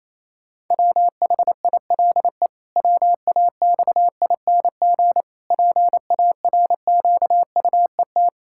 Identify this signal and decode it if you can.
Morse code, heard as a keyed tone, about 28 wpm, 700 Hz. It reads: WHILE WAXING PARQUET
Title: Pangram Challenge v2 - 28wpm